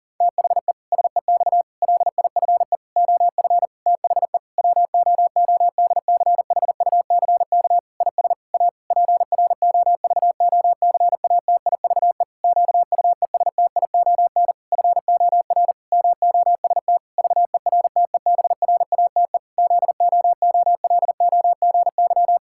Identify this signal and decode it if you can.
Morse code, heard as a keyed tone, about 40 wpm, 700 Hz. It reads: THE SEX LIFE OF THE WOODCHUCK IS A PROVOCATIVE QUESTION FOR MOST VERTEBRATE ZOOLOGY
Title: Pangram Challenge v2 - 40wpm